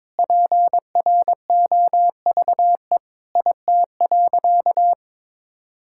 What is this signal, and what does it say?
Morse code, heard as a keyed tone, about 22 wpm, 700 Hz. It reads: PROVE IT.
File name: practice / 22wpm / Sentences with 8-Word Phrases - 22wpm